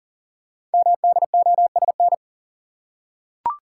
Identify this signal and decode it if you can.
Morse code, heard as a keyed tone, about 38 wpm, 690 Hz. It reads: MDOSN
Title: Code Groups - 5 Letter:Numbers - 40wpm